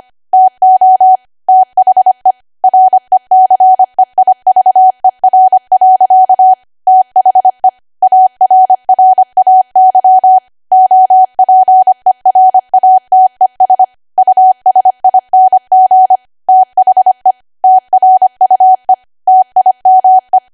Morse code, heard as a keyed tone, about 25 wpm, 750 Hz. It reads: TO THE RECEIVER. THE ARRAY OPERATES USING THE TRUE TIME
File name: Archive / 25 WPM / 25WPM